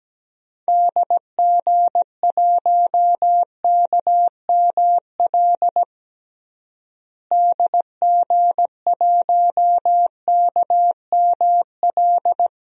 Morse code, heard as a keyed tone, about 17 wpm, 700 Hz. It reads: DG1KML DG1KML
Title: DXCC-100 Call Signs - 6 Characters - Repeated 3x - 17wpm